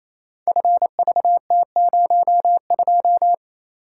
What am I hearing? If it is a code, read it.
Morse code, heard as a keyed tone, about 27 wpm, 700 Hz. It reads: FVT02
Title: Code Groups - 5 Letter:Numbers - 28wpm